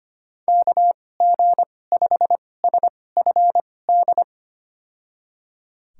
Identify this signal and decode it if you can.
Morse code, heard as a keyed tone, about 25 wpm, spaced as for 18 wpm, 700 Hz. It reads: KG5SFD
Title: USA Call Signs - 6 Characters - 2x Character Spacing - 25wpm